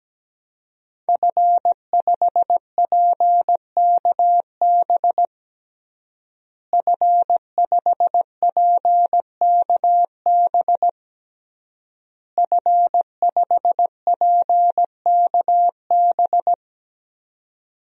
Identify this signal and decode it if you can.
Morse code, heard as a keyed tone, about 17 wpm, 700 Hz. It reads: F5PKB F5PKB F5PKB
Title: DXCC-100 Call Signs - 5 Characters - Repeated 3x - 17wpm